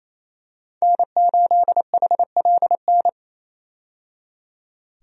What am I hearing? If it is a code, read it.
Morse code, heard as a keyed tone, about 27 wpm, 700 Hz. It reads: N8HLN
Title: USA Call Signs - 5 Characters - 28wpm